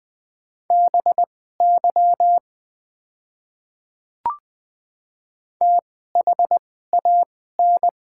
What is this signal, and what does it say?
Morse code, heard as a keyed tone, about 21 wpm, spaced as for 15 wpm, 700 Hz. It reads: BY THAN
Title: 2x Character Spacing - Top 300 Words - 20wpm